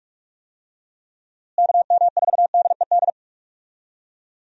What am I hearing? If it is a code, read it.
Morse code, heard as a keyed tone, about 44 wpm, 700 Hz. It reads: KM4DED
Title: USA Call Signs - 6 Characters - Repeated 3x - 45wpm